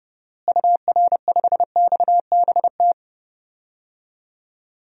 Morse code, heard as a keyed tone, about 30 wpm, 700 Hz. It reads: UR5XBT
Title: DXCC-100 Call Signs - 6 Characters - Repeated 3x - 30wpm